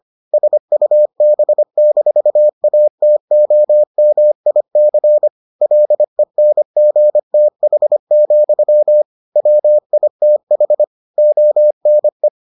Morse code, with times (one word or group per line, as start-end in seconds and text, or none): 0.3-5.3 SUB-ATOMIC
5.6-9.0 LENGTH,
9.4-10.8 WITH
11.2-12.3 ONE